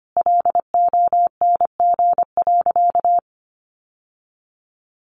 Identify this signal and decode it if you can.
Morse code, heard as a keyed tone, about 25 wpm, 700 Hz. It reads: LONG.
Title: Sentences from Top 600 Words 25wpm